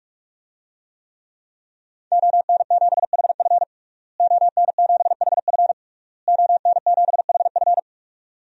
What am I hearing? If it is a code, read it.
Morse code, heard as a keyed tone, about 46 wpm, 700 Hz. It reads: ON7HF ON7HF ON7HF
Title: DXCC-100 Call Signs - 5 Characters - Repeated 3x - 45wpm